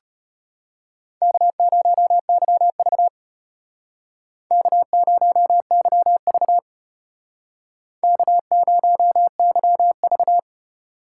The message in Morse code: K0YV K0YV K0YV

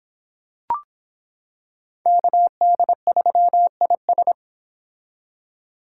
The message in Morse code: KD3IS

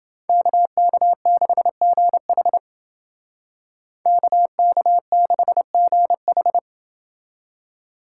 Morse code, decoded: KK6GH KK6GH

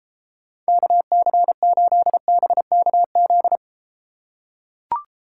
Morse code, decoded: KC8BKZ

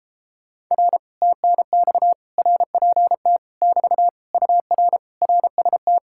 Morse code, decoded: R TNX RPT <BT> UR RST

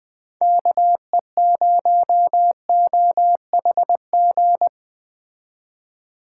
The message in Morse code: KE0OHG

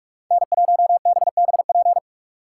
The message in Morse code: N1BBP

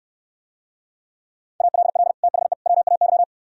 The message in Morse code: ALL THE WAY

